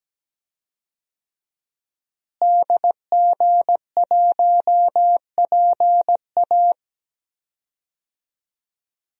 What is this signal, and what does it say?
Morse code, heard as a keyed tone, about 17 wpm, 700 Hz. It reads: DG1PA
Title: DXCC-100 Call Signs - 5 Characters - Repeated 2x - 17wpm